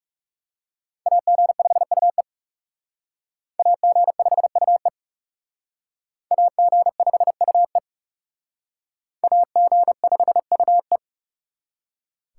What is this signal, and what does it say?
Morse code, heard as a keyed tone, about 38 wpm, 700 Hz. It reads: AG5UE AG5UE AG5UE AG5UE